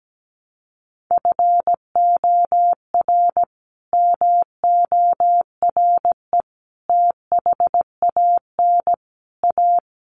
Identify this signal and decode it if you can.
Morse code, heard as a keyed tone, about 17 wpm, 700 Hz. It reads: FOR MORE THAN A
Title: Sets of 5 Words 17wpm